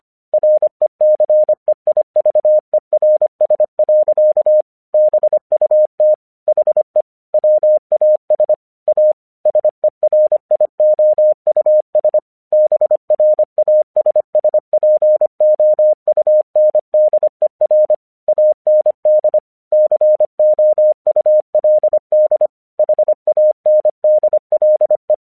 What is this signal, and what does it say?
Morse code, heard as a keyed tone, about 25 wpm, 610 Hz. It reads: RECEIVERS. BUT HE WAS A SERIOUS BRASSPOUNDER AND COULD HANDLE
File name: 1 English 25wpm / AQC11